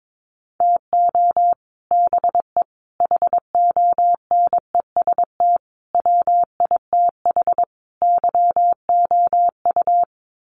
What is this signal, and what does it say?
Morse code, heard as a keyed tone, about 22 wpm, 700 Hz. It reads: TO BE HONEST WITH YOU
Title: Sets of 5 Words Encore 22wpm